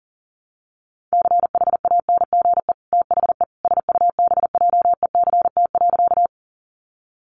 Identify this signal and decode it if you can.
Morse code, heard as a keyed tone, about 40 wpm, 700 Hz. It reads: CHANGE THE SUBJECT.
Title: Sentences from Top 800 Words 40wpm